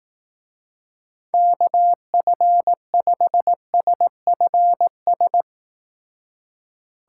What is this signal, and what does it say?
Morse code, heard as a keyed tone, about 18 wpm, 700 Hz. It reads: KF5SFS
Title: Speed Racing - USA 2x3 Call Signs 15wpm